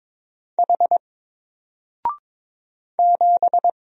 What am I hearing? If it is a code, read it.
Morse code, heard as a keyed tone, about 22 wpm, 700 Hz. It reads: H 7
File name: Single-Letter-Number-22wpm